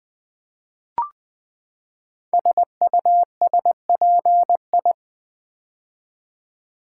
Morse code, heard as a keyed tone, about 20 wpm, 700 Hz. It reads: SUSPI
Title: Build A Word - 5 Letters At A Time - 10-Letter Words - 20wpm